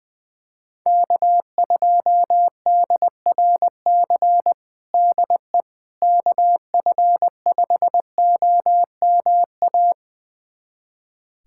Morse code, heard as a keyed tone, about 20 wpm, 700 Hz. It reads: K2DRC DE KF5OMA
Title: USA Callsign DE Callsign 20wpm